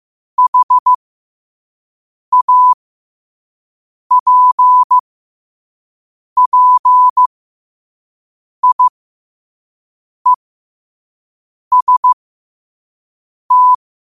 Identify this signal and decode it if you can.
Morse code, heard as a keyed tone, about 15 wpm, spaced as for 5 wpm, 1000 Hz. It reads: HAPPIEST